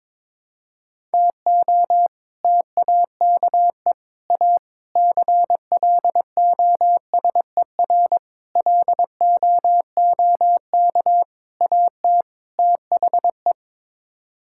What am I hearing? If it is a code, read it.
Morse code, heard as a keyed tone, about 22 wpm, 700 Hz. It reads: TO TAKE A CLOSER LOOK AT THE
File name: practice / 22wpm / Sets of 7 Words - 22wpm